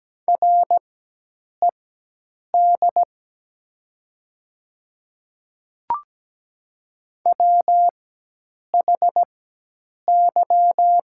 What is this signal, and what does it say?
Morse code, heard as a keyed tone, about 17 wpm, spaced as for 8 wpm, 700 Hz. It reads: RED WHY
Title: 4x Character Spacing - Top 500 Words - 17wpm